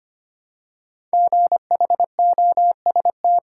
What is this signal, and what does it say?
Morse code, heard as a keyed tone, about 25 wpm, 700 Hz. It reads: GHOST
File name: 5 Letter Words - First 3 Letters Given - 25wpm